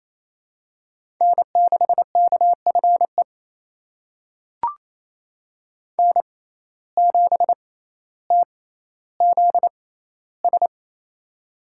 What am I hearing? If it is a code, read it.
Morse code, heard as a keyed tone, about 28 wpm, spaced as for 14 wpm, 700 Hz. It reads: N6KFE N7TZS
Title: USA Call Signs - 5 Characters - 6x Character Spacing - 28wpm